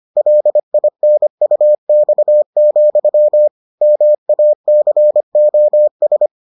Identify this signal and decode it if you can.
Morse code, heard as a keyed tone, about 25 wpm, 600 Hz. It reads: LINUX, MACOS